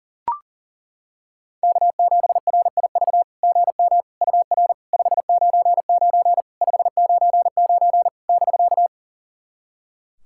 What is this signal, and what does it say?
Morse code, heard as a keyed tone, about 40 wpm, 690 Hz. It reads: K7RIV GM UR 599 599 <BK>
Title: SOTA QSO Practice 40wpm